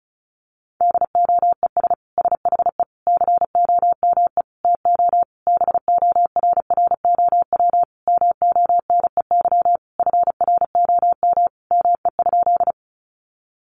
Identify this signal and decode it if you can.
Morse code, heard as a keyed tone, about 34 wpm, 700 Hz. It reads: DOES SHE COME TO BORROW MONEY FROM ME?
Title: Sentences from Top 1500 Words 35wpm